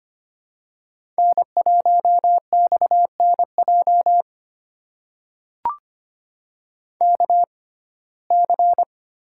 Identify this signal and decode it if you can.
Morse code, heard as a keyed tone, about 25 wpm, spaced as for 18 wpm, 700 Hz. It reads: N1XNJ KC
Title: USA Call Signs - 6x Character Spacing - 25wpm